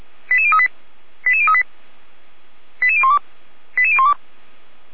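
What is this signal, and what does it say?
Five-tone selective call (ZVEI), heard as quick runs of five tones, tones of about 70 ms. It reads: 88038 88038 88012 88012
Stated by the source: ZVEI five-tone call received on an SDR.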